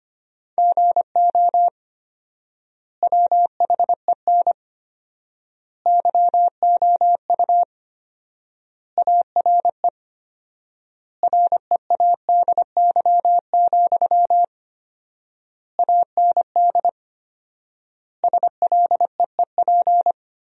Morse code, heard as a keyed tone, about 25 wpm, 700 Hz. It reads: GO WHEN YOU ARE READY, AND SLEEP
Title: 4x Word-Spacing - Sentences from Top 500 Words - 25wpm